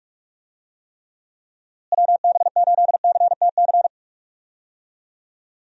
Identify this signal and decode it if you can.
Morse code, heard as a keyed tone, about 46 wpm, 700 Hz. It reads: WB8CTC
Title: USA Call Signs - 6 Characters - Repeated 3x - 45wpm